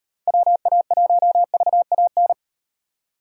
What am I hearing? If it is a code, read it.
Morse code, heard as a keyed tone, about 37 wpm, 700 Hz. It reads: WA1VAN